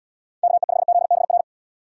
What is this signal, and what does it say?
Morse code, heard as a keyed tone, about 79 wpm, 700 Hz. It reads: KE5ZBR